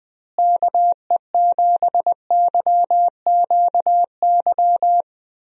Morse code, heard as a keyed tone, about 20 wpm, 700 Hz. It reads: KE7YQY
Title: USA Call Signs - 6 Characters - Repeated 3x - 20wpm